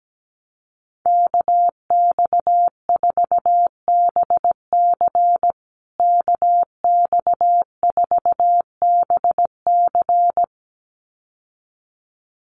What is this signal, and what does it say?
Morse code, heard as a keyed tone, about 17 wpm, 700 Hz. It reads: KX4BC KX4BC
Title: USA Call Signs Repeated 2x 17wpm